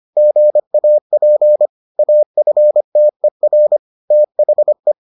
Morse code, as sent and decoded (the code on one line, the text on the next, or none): --. .- .--. / .- ..-. - . .-. / - .... .
GAP AFTER THE